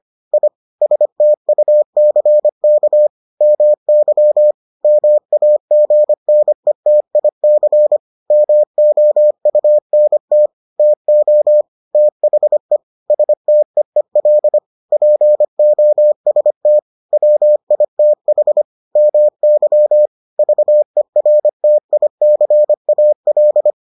Morse code, heard as a keyed tone, about 25 wpm, 600 Hz. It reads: I STUCK MY MAGNETIC MOUNT TO THE STEEL POST WITH MY VERTICAL